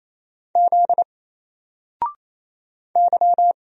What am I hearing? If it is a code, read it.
Morse code, heard as a keyed tone, about 27 wpm, 700 Hz. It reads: Z Y
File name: Single-Letter-Number-28wpm